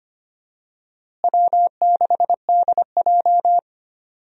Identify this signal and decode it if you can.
Morse code, heard as a keyed tone, about 25 wpm, 700 Hz. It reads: W6DJ
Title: USA Call Signs - 4 Characters - Repeated 3x - 25wpm